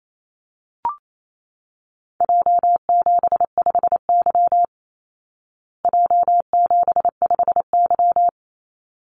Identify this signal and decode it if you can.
Morse code, heard as a keyed tone, about 28 wpm, 700 Hz. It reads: J75Y J75Y
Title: Call Signs Repeated 3x - 28wpm